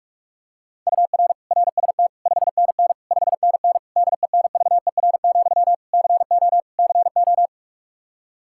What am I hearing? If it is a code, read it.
Morse code, heard as a keyed tone, about 46 wpm, 700 Hz. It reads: UR RST 5NN 5NN DENVER, CO CO
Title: Practice QSO - Exchange 1 - Beginning - 45wpm